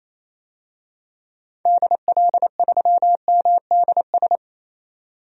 Morse code, heard as a keyed tone, about 28 wpm, 700 Hz. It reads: DL3MDS
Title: DXCC-100 Call Signs - 6 Characters - 28wpm